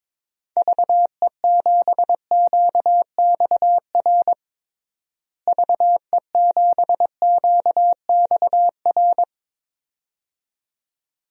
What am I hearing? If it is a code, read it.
Morse code, heard as a keyed tone, about 21 wpm, 700 Hz. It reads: VE7QXR VE7QXR
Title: DXCC-100 Call Signs - 6 Characters - Repeated 2x - 22wpm